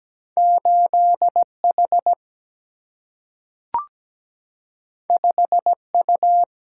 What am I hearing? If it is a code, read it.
Morse code, heard as a keyed tone, about 17 wpm, 700 Hz. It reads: 8H 5U
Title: Single-Letter Single-Number 17wpm